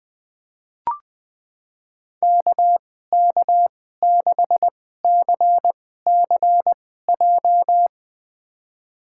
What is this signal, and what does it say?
Morse code, heard as a keyed tone, about 20 wpm, spaced as for 14 wpm, 700 Hz. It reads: KK6CCJ
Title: USA Call Signs - 2x Character Spacing - 20wpm